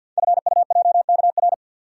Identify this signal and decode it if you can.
Morse code, heard as a keyed tone, about 50 wpm, 700 Hz. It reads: FUJKR
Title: Code Groups - 5 Letters - 50wpm